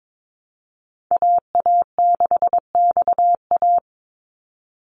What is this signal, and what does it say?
Morse code, heard as a keyed tone, about 22 wpm, 700 Hz. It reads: AA6XA
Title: NA SOTA Activator Callsigns 22wpm